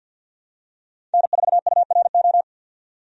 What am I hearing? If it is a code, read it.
Morse code, heard as a keyed tone, about 51 wpm, 700 Hz. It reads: N4URO
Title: USA Call Signs - 5 Characters - 50wpm